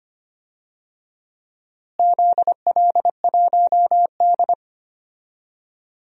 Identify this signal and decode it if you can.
Morse code, heard as a keyed tone, about 25 wpm, 700 Hz. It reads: ZL1D
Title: DXCC-100 Call Signs - 4 Characters - Repeated 3x - 25wpm